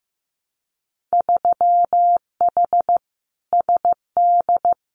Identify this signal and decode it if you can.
Morse code, heard as a keyed tone, about 15 wpm, 700 Hz. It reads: 3H SD
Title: Winter Field Day Exchanges Encore 15wpm